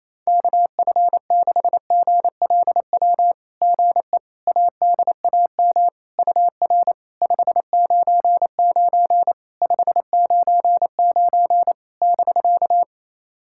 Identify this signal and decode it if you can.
Morse code, heard as a keyed tone, about 28 wpm, 700 Hz. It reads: KF6GLW GE ADAM UR 599 599 <BK>
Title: SOTA QSO Practice 28wpm